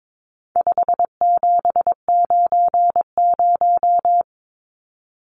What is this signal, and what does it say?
Morse code, heard as a keyed tone, about 22 wpm, 700 Hz. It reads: 5790
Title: Numbers - 4-Digit 22wpm